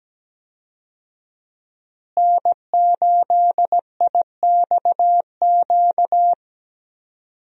Morse code, heard as a keyed tone, about 17 wpm, 700 Hz. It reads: N8IXQ
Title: USA Call Signs - 5 Characters - Repeated 2x - 17wpm